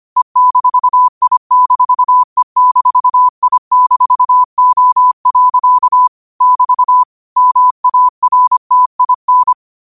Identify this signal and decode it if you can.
Morse code, heard as a keyed tone, about 25 wpm, 1000 Hz. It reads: E-I-E-I-O. <BT> MARTIN